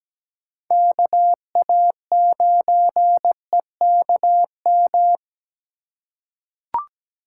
Morse code, heard as a keyed tone, about 17 wpm, 700 Hz. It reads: KA9EKM
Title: USA Call Signs - 6 Characters - 3x Character Spacing - 17wpm